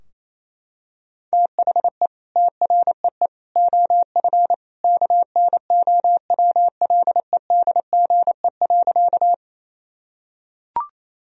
Morse code, heard as a keyed tone, about 28 wpm, 700 Hz. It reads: THE TREE OF KNOWLEDGE.
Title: Sentences from Top 1000 Words - Review Part 3 - 28wpm